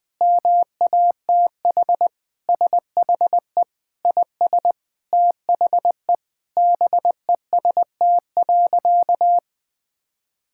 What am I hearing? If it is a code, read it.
Morse code, heard as a keyed tone, about 20 wpm, 700 Hz. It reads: MATH SHE IS THE BEST.
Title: Sentences with 3-Word Phrases - 20wpm